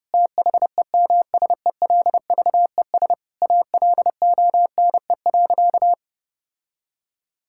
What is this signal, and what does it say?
Morse code, heard as a keyed tone, about 30 wpm, 700 Hz. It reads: THEMSELVES ALONE.
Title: Sentences from Top 1000 Words - Review Part 5 - 30wpm